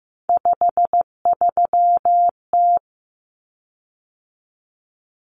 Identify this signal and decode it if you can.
Morse code, heard as a keyed tone, about 15 wpm, 700 Hz. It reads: 53T
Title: Common Cut Numbers - 3-Digit 15wpm